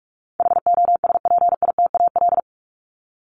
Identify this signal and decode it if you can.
Morse code, heard as a keyed tone, about 44 wpm, 700 Hz. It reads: HOSPITAL
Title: Top 1751-2000 Words 45wpm